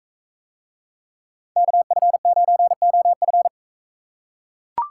Morse code, heard as a keyed tone, about 42 wpm, 700 Hz. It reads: KF9OF